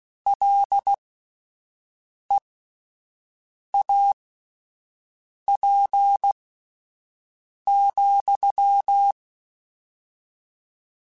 Morse code, heard as a keyed tone, about 16 wpm, spaced as for 5 wpm, 800 Hz. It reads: LEAP,